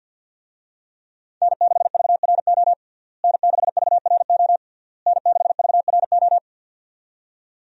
Morse code, heard as a keyed tone, about 48 wpm, 700 Hz. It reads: N6VRO N6VRO N6VRO
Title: USA Call Signs - 5 Characters - Repeated 3x - 50wpm